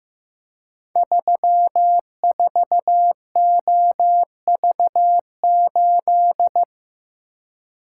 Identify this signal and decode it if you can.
Morse code, heard as a keyed tone, about 15 wpm, 700 Hz. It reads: 34OV8
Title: Code Groups - 5 Letter:Numbers - 15wpm